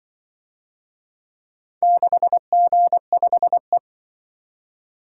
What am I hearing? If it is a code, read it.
Morse code, heard as a keyed tone, about 24 wpm, 700 Hz. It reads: 6G5E
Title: Speed-Racing - Four Letter-Number Combinations - 20wpm